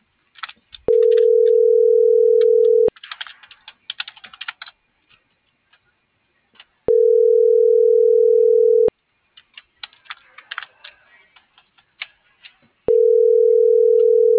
Ringback tone (North America), 440 + 480 Hz.